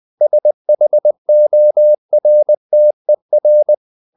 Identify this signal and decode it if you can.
Morse code, heard as a keyed tone, about 20 wpm, 600 Hz. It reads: SHORTER
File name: the old man and the sea part 2 20wpm